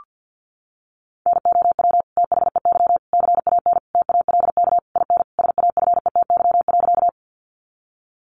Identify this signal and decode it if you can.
Morse code, heard as a keyed tone, about 48 wpm, 700 Hz. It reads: NOW THEY CAN TALK IN SAFETY.
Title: Sentences from Top 700 Words 50wpm